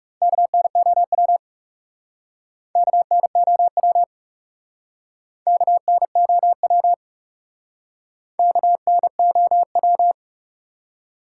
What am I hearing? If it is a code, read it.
Morse code, heard as a keyed tone, about 37 wpm, 700 Hz. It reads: KNOW KNOW KNOW KNOW
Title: Speed-Racing Top 180 QSO Elements 30wpm